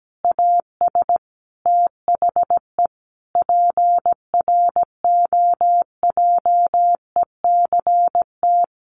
Morse code, heard as a keyed tone, about 17 wpm, 700 Hz. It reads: AS THE PROJECT
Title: Sentences from Top 1400 Words 17wpm